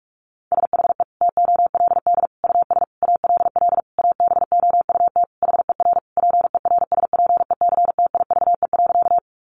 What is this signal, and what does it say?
Morse code, heard as a keyed tone, about 44 wpm, 700 Hz. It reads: SHE TOLD US ALL ABOUT HER PERSPECTIVE.
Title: Sentences from Top 1200 Words 45wpm